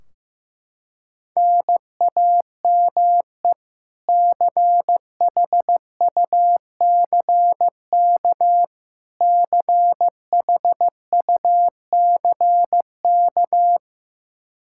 Morse code, heard as a keyed tone, about 15 wpm, 700 Hz. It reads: NAME CHUCK CHUCK
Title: K1USN SST QSO Practice 15wpm